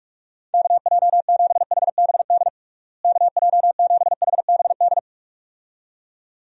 Morse code, heard as a keyed tone, about 46 wpm, 700 Hz. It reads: KJ7HBD KJ7HBD
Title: USA Call Signs - 6 Characters - Repeated 2x - 45wpm